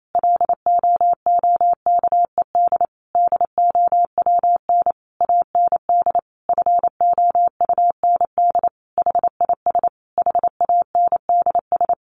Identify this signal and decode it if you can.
Morse code, heard as a keyed tone, about 28 wpm, 700 Hz. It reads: LOOKED DOWN AND FOUND HIS HANDS
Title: A Sound Of Thunder 28wpm